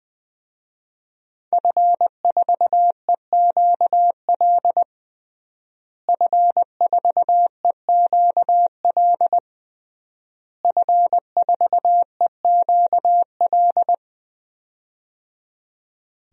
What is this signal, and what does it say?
Morse code, heard as a keyed tone, about 20 wpm, 700 Hz. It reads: F4EQL F4EQL F4EQL
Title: DXCC-100 Call Signs - 5 Characters - Repeated 3x - 20wpm